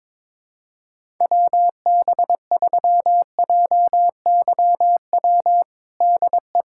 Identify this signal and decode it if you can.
Morse code, heard as a keyed tone, about 22 wpm, 700 Hz. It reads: WB3JYW DE
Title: USA Callsign DE Callsign - 6 Characters - 22wpm